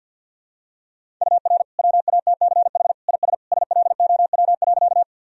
Morse code, heard as a keyed tone, about 52 wpm, 700 Hz. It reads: UR WATCH IS SLOW.
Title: Sentences from Top 400 Words and Abbreviations 50wpm